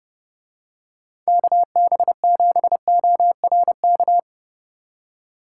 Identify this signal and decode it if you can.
Morse code, heard as a keyed tone, about 30 wpm, 700 Hz. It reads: KB7ORK